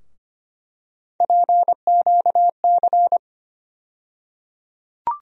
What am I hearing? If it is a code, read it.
Morse code, heard as a keyed tone, about 25 wpm, 700 Hz. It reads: PQC
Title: USA Call Sign Postfixes - 3 Letters 25wpm